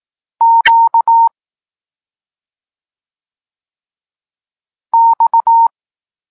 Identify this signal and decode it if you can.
Morse code, heard as a keyed tone, about 18 wpm, 920 Hz. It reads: Q X
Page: Tekst 32 – Doskonalenie odbioru poznanych liter